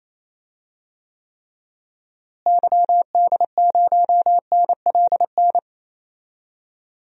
Morse code, heard as a keyed tone, about 28 wpm, 700 Hz. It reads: YD0NLN
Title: DXCC-100 Call Signs - 6 Characters - Repeated 3x - 28wpm